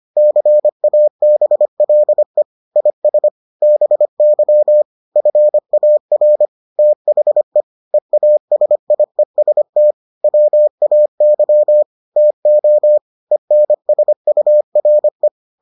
Morse code, heard as a keyed tone, about 25 wpm, 600 Hz. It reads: CABLE IS BY FAR THE EASIEST WAY TO ENSURE